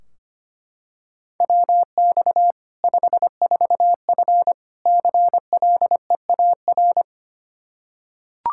Morse code, heard as a keyed tone, about 25 wpm, 700 Hz. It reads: WX 54F CLEAR
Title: K1USN SST QSO Practice 25wpm